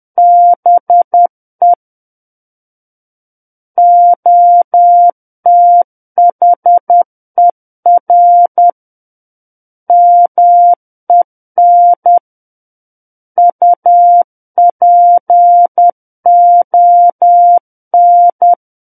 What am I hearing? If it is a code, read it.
Morse code, heard as a keyed tone, about 10 wpm, 700 Hz. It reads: BE OTHER MEN UPON
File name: Book One / 01-Chapter 1 - 10 WPM